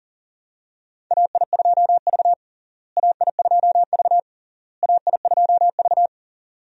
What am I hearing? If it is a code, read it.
Morse code, heard as a keyed tone, about 40 wpm, 700 Hz. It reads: AI2V AI2V AI2V
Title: USA Call Signs - 4 Characters - Repeated 3x - 40wpm